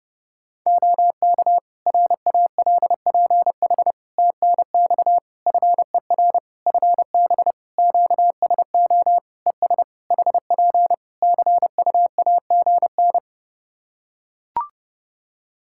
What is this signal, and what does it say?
Morse code, heard as a keyed tone, about 32 wpm, 700 Hz. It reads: OK RALPH TNX FER FB QSO ES HP CUAGN
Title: Practice QSO - Exchange 4 - Ending - 30wpm